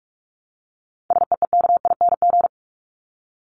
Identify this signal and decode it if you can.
Morse code, heard as a keyed tone, about 44 wpm, 700 Hz. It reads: SEEKING
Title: ING Suffix - Top 100 Words with Base Words 45wpm